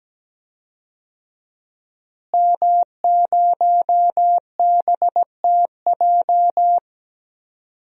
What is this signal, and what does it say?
Morse code, heard as a keyed tone, about 17 wpm, 700 Hz. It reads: M0BTJ
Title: DXCC-100 Call Signs - 5 Characters - Repeated 3x - 17wpm